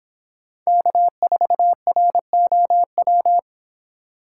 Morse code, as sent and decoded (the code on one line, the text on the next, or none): -.- ....- .-. --- .--
K4ROW